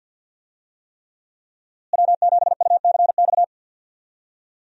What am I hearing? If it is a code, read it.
Morse code, heard as a keyed tone, about 49 wpm, 700 Hz. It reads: W7UCX
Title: USA Call Signs - 5 Characters - Repeated 3x - 50wpm